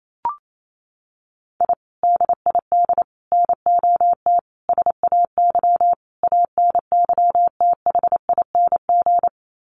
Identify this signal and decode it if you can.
Morse code, heard as a keyed tone, about 28 wpm, 700 Hz. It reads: I DID NOT SAY ANYTHING
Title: Sets of 5 Words Encore2 28wpm